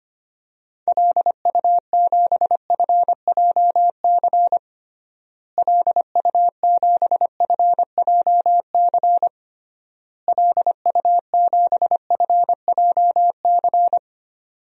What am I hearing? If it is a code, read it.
Morse code, heard as a keyed tone, about 25 wpm, 700 Hz. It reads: LU7FJC LU7FJC LU7FJC
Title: DXCC-100 Call Signs - 6 Characters - Repeated 3x - 25wpm